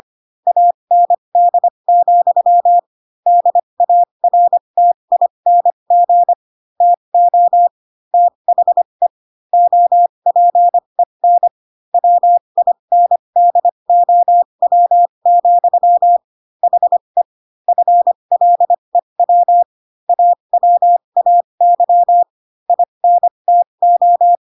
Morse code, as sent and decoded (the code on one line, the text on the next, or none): .- -. -.. --..-- / -.. .- .-. - .. -. --. / - --- / - .... / --- .--. -. / .-- .. -. -.. --- .-- --..-- / .... / ..-. .-.. .-- / .- .-- .- -.-- / .. -. - ---
AND, DARTING TO THE OPEN WINDOW, HE FLEW AWAY INTO